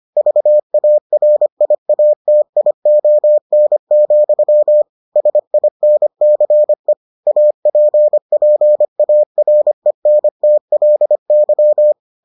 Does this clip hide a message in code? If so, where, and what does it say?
Yes: Morse code from 0.2 to 11.9 s, heard as a keyed tone, about 25 wpm, 600 Hz. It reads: VARIATION, SINCE APPARENTLY